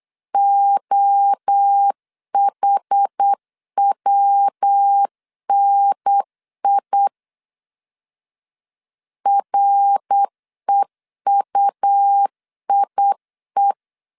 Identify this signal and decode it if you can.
Morse code, heard as a keyed tone, about 8 wpm, 800 Hz. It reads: OHWNI REUIE
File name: CW / DET / 1 3 1500C S B 8wpm 4141